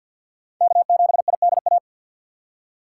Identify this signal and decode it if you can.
Morse code, heard as a keyed tone, about 53 wpm, 700 Hz. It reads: K7IDA
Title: USA Call Signs - 5 Characters - 2x Character Spacing - 50wpm